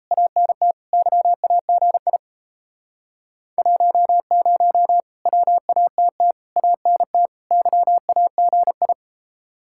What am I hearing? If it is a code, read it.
Morse code, heard as a keyed tone, about 37 wpm, 700 Hz. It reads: ANT YAGI 10 WATT ANT YAGI